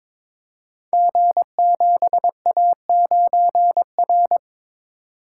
Morse code, heard as a keyed tone, about 22 wpm, 700 Hz. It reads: G7A9R
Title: Code Groups - 5 Letter:Numbers - 22wpm